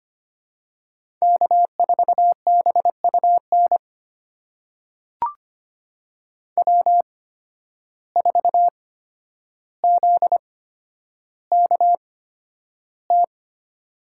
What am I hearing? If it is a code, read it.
Morse code, heard as a keyed tone, about 25 wpm, spaced as for 11 wpm, 700 Hz. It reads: K4BUN W4ZKT
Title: USA Call Signs - 5 Characters - 8x Character Spacing - 25wpm